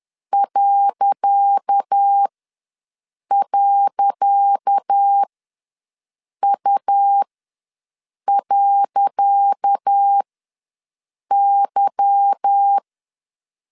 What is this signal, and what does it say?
Morse code, heard as a keyed tone, about 10 wpm, spaced as for 6 wpm, 800 Hz. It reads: ..U.Y
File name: LEZIONI / 10 SIGNS / 10C DOT ws 10wpm 4213